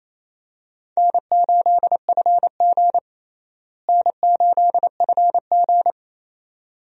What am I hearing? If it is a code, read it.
Morse code, heard as a keyed tone, about 28 wpm, 700 Hz. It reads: N8FG N8FG
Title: USA Call Signs - 4 Characters - Repeated 2x - 28wpm